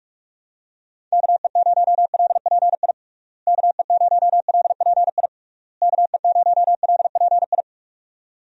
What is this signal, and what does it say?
Morse code, heard as a keyed tone, about 46 wpm, 700 Hz. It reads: KE0LPI KE0LPI KE0LPI
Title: USA Call Signs - 6 Characters - Repeated 3x - 45wpm